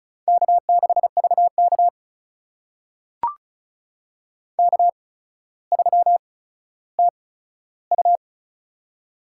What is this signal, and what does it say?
Morse code, heard as a keyed tone, about 35 wpm, spaced as for 15 wpm, 700 Hz. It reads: K6VK K3TU
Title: USA Call Signs - 4 Characters - 8x Character Spacing - 35wpm